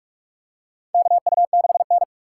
Morse code, heard as a keyed tone, about 42 wpm, 700 Hz. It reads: KU6N
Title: USA Call Signs - 4 Characters - 4x Character Spacing - 45wpm